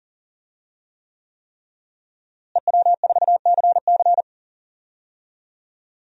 Morse code, heard as a keyed tone, about 40 wpm, 700 Hz. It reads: EW4CC
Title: DXCC-100 Call Signs - 5 Characters - Repeated 3x - 40wpm